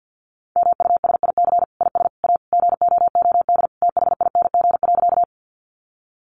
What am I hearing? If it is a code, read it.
Morse code, heard as a keyed tone, about 49 wpm, 700 Hz. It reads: MUSIC IS A GOOD THING.
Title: Sentences from Top 300 Words 50wpm